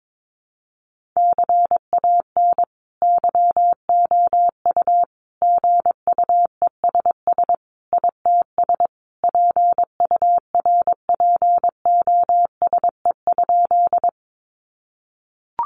Morse code, heard as a keyed tone, about 22 wpm, 700 Hz. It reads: CAN YOU GUESS ITS PURPOSE?
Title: Sentences from Top 1500 Words 22wpm